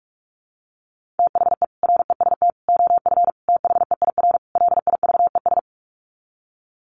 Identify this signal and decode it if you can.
Morse code, heard as a keyed tone, about 46 wpm, 700 Hz. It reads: THE REST OF THEIR LIVES
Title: Sets of 5 Words Encore 45wpm